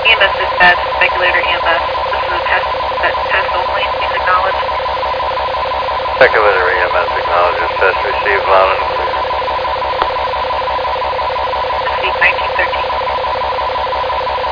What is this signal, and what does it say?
Two-tone teleprinter signal, about 450 Hz shift, with speech.